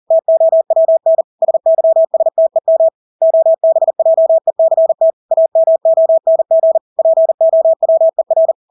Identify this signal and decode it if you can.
Morse code, heard as a keyed tone, about 40 wpm, 650 Hz. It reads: TOWN SYSTEM OBJECT AMONG POWER
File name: Engelske ord 40-40 wpm